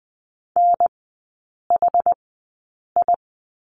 Morse code, heard as a keyed tone, about 20 wpm, spaced as for 8 wpm, 700 Hz. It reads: NHI
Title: 067 Lesson 12e - Letter d - Three Characters 20wpm